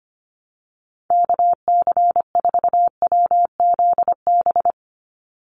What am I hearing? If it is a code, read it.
Morse code, heard as a keyed tone, about 25 wpm, 700 Hz. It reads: KC4WZB
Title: NA SOTA Activator Callsigns 25wpm